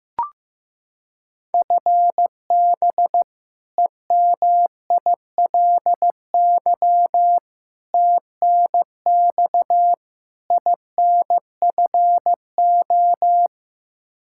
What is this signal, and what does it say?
Morse code, heard as a keyed tone, about 15 wpm, 700 Hz. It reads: FB EMILY TNX INFO
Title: Practice QSO - Exchange 3 - More Info - 15wpm